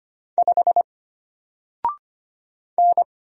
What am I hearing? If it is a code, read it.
Morse code, heard as a keyed tone, about 25 wpm, 700 Hz. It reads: 5 N
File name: Single-Letter-Number-25wpm